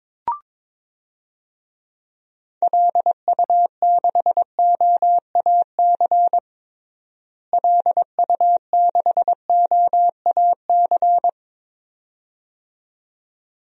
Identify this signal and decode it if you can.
Morse code, heard as a keyed tone, about 22 wpm, 700 Hz. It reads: LU6OAC LU6OAC
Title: DXCC-100 Call Signs - 6 Characters - Repeated 2x - 22wpm